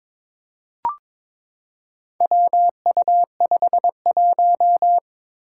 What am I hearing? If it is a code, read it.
Morse code, heard as a keyed tone, about 22 wpm, 700 Hz. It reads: WU51
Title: Four Challenging Letter-Number Combinations 22wpm